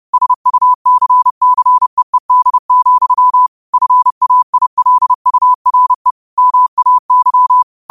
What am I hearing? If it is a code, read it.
Morse code, heard as a keyed tone, about 30 wpm, 1000 Hz. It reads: SUCCEED, FAILURE MAY